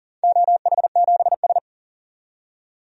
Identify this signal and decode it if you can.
Morse code, heard as a keyed tone, about 38 wpm, 700 Hz. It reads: OH7S
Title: DXCC-100 Call Signs - 4 Characters - 40wpm